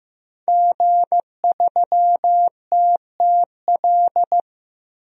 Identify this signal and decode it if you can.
Morse code, heard as a keyed tone, about 15 wpm, 700 Hz. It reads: G3TTL
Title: DXCC-100 Call Signs - 5 Characters - Repeated 3x - 15wpm